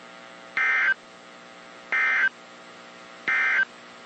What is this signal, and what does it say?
Emergency Alert System (SAME) tones, heard as harsh buzzing bursts. It is heard from 0.4 to 3.7 s.
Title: WZ2523 - NOAA Weather Radio